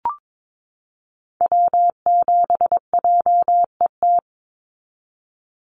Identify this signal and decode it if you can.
Morse code, heard as a keyed tone, about 22 wpm, 700 Hz. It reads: W7JET
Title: NA SOTA Activator Callsigns 22wpm